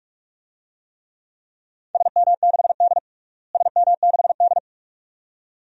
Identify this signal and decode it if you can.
Morse code, heard as a keyed tone, about 46 wpm, 700 Hz. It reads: SM6D SM6D
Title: DXCC-100 Call Signs - 4 Characters - Repeated 2x - 45wpm